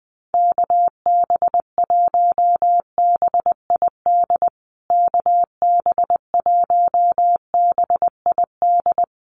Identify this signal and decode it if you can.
Morse code, heard as a keyed tone, about 19 wpm, 700 Hz. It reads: KB1BID KB1BID
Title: USA Call Signs Repeated 2x 20wpm